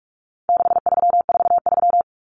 Morse code, heard as a keyed tone, about 44 wpm, 700 Hz. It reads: 6343
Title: Common Cut Numbers - 4-Digit 45wpm